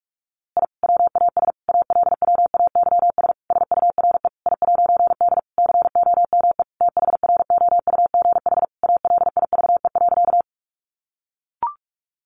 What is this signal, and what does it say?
Morse code, heard as a keyed tone, about 46 wpm, 700 Hz. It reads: I WAS ALWAYS SURE I'D COME THROUGH ALIVE.
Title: A Sound Of Thunder 45wpm